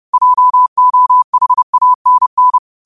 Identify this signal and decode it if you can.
Morse code, heard as a keyed tone, about 30 wpm, 1000 Hz. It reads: JOHANN
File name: qotdfiles / 30wpm